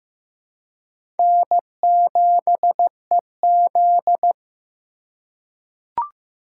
Morse code, heard as a keyed tone, about 15 wpm, 700 Hz. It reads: N7EZ
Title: USA Call Signs - 4 Characters - 3x Character Spacing - 15wpm